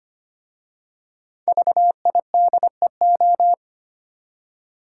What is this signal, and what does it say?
Morse code, heard as a keyed tone, about 25 wpm, 700 Hz. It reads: VIDEO
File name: 5 Letter Words - First 3 Letters Given - 25wpm